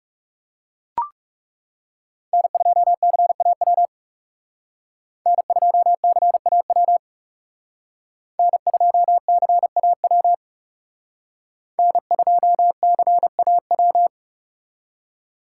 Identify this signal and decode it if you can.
Morse code, heard as a keyed tone, about 37 wpm, 700 Hz. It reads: N2CAW N2CAW N2CAW N2CAW
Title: Speed Racing - USA 1x3 Call Signs 30wpm